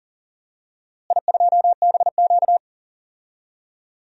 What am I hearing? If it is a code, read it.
Morse code, heard as a keyed tone, about 39 wpm, 700 Hz. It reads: I2BQ
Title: DXCC-100 Call Signs - 4 Characters - Repeated 3x - 40wpm